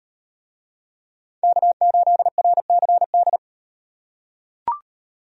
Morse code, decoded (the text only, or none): K8RCD